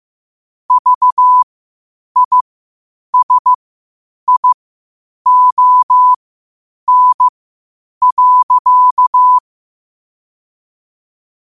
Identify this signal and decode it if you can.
Morse code, heard as a keyed tone, about 15 wpm, spaced as for 8 wpm, 1000 Hz. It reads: VISION.
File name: qotdfiles / 8wpm